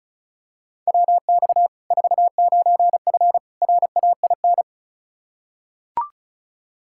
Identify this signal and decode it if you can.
Morse code, heard as a keyed tone, about 34 wpm, 700 Hz. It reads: WX 49F RAIN
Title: Practice QSO - Exchange 2 - Info - 35wpm